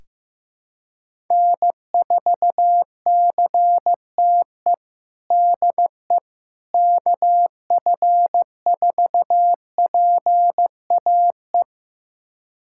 Morse code, heard as a keyed tone, about 16 wpm, 700 Hz. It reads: N4CTE DE KF4PAE
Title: USA Callsign DE Callsign 15wpm